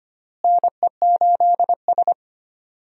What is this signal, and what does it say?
Morse code, heard as a keyed tone, about 25 wpm, 700 Hz. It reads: NE8S
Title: USA Call Signs - 4 Characters - Repeated 3x - 25wpm